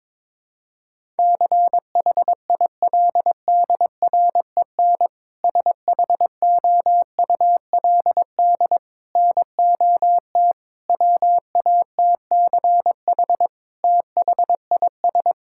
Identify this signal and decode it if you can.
Morse code, heard as a keyed tone, about 22 wpm, 700 Hz. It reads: CHILDREN SHOULD NOT WATCH THIS
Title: Sentences from Top 1000 Words - Review Part 5 - 22wpm